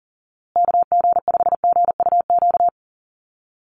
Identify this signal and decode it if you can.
Morse code, heard as a keyed tone, about 40 wpm, 700 Hz. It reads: KG5GUQ
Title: USA Call Signs Repeated 3x 40wpm